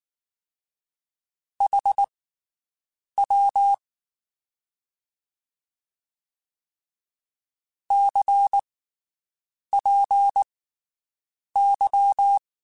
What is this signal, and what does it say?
Morse code, heard as a keyed tone, about 19 wpm, spaced as for 7 wpm, 800 Hz. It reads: HW CPY
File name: QSOs Spaced